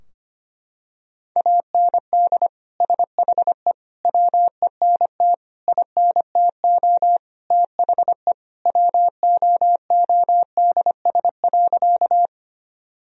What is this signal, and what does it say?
Morse code, heard as a keyed tone, about 25 wpm, 700 Hz. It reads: AND SHE WENT INTO THE WOODS.
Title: Sentences from Top 1000 Words - Review Part 2 - 25wpm